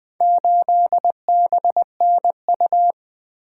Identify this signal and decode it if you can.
Morse code, heard as a keyed tone, about 20 wpm, 700 Hz. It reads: 8BNU
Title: Four Letter-Number Combinations - 20wpm